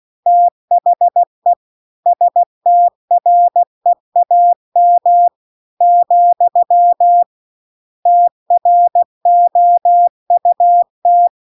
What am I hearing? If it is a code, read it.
Morse code, heard as a keyed tone, about 16 wpm, 700 Hz. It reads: THE STREAM , TROUT